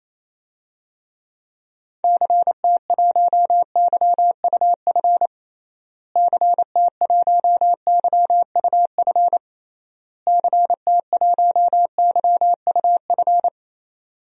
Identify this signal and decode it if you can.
Morse code, heard as a keyed tone, about 28 wpm, 700 Hz. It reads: CT1YUF CT1YUF CT1YUF
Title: DXCC-100 Call Signs - 6 Characters - Repeated 3x - 28wpm